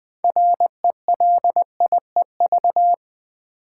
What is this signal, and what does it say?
Morse code, heard as a keyed tone, about 20 wpm, 700 Hz. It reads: RELIEV
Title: Build A Word - 2 Letters At A Time - 8-Letter Words - 20wpm